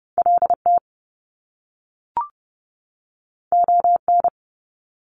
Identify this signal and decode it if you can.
Morse code, heard as a keyed tone, about 30 wpm, 700 Hz. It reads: LT ON
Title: Top Two Letter Combinations in Words 30wpm